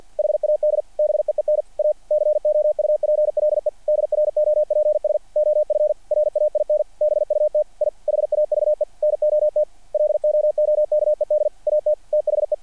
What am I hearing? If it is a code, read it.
Morse code, heard as a keyed tone, about 48 wpm, 600 Hz. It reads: HAD BEEN A COUPLE DROPS OF RAIN BUT I HAVE NOT LOOKED AT THE
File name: archived / CRSnet